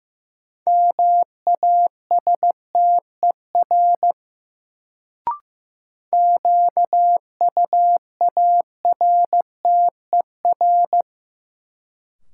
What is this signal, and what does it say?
Morse code, heard as a keyed tone, about 15 wpm, 700 Hz. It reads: MASTER QUARTER
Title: Sentences from Top 1300 Words - New Word Drill - 15wpm